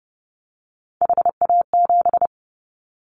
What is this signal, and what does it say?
Morse code, heard as a keyed tone, about 30 wpm, 700 Hz. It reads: HA7
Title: Callsign Prefixes for Top 25 DXCC - 30wpm